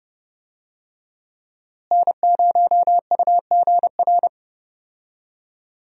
Morse code, heard as a keyed tone, about 30 wpm, 700 Hz. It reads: N0UGR
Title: USA Call Signs - 5 Characters - Repeated 3x - 30wpm